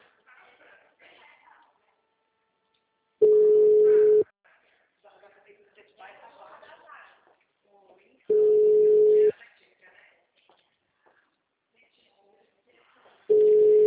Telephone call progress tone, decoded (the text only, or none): ringback tone